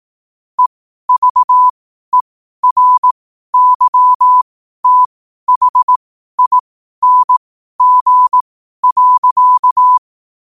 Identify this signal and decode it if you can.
Morse code, heard as a keyed tone, about 18 wpm, spaced as for 12 wpm, 1000 Hz. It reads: EVERYTHING.
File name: qotdfiles / 12wpm